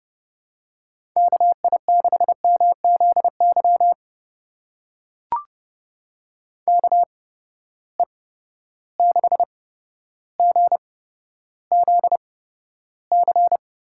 Morse code, heard as a keyed tone, about 30 wpm, spaced as for 13 wpm, 700 Hz. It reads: KI6MZY KE6GZC
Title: USA Call Signs - 6 Characters - 8x Character Spacing - 30wpm